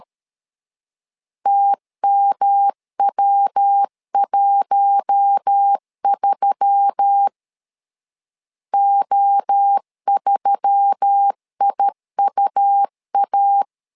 Morse code, heard as a keyed tone, about 13 wpm, 800 Hz. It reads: TMW13 O3IUA
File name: LEZIONI / 4 Q1S3 12wpm 2023